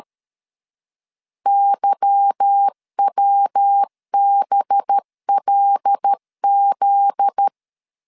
Morse code, heard as a keyed tone, about 13 wpm, 800 Hz. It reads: YWBLZ
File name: LEZIONI / 8 YJ45 / 8 Y 12wpm 2228